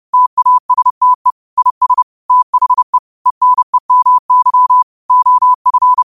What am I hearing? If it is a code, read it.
Morse code, heard as a keyed tone, about 30 wpm, 1000 Hz. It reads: TASTE IS THE ENEMY OF
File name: qotdfiles / 30wpm